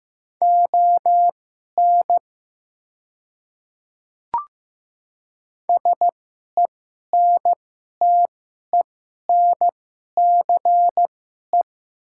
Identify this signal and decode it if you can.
Morse code, heard as a keyed tone, about 15 wpm, spaced as for 11 wpm, 700 Hz. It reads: ON SENTENCE
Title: 2x Character Spacing - Top 200 Words - 15wpm